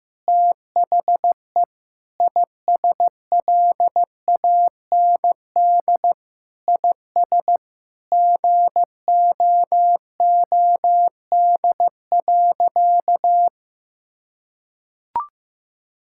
Morse code, heard as a keyed tone, about 15 wpm, 700 Hz. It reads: THE ISLAND IS GOOD.
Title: Sentences from Top 1000 Words - Review Part 6 - 15wpm